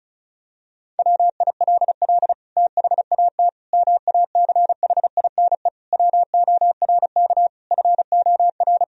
Morse code, heard as a keyed tone, about 34 wpm, 700 Hz. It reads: WILL THAT MACHINE WORK FOR
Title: Sentences from Top 1000 Words - Review Part 2 - 35wpm